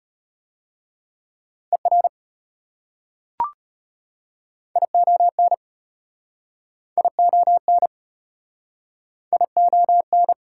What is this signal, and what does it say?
Morse code, heard as a keyed tone, about 31 wpm, 700 Hz. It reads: ER ION ION ION